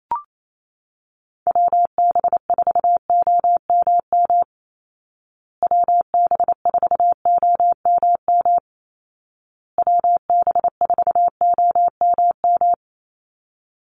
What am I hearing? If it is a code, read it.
Morse code, heard as a keyed tone, about 28 wpm, 700 Hz. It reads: WB4OMM WB4OMM WB4OMM
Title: Call Signs Repeated 2x - 28wpm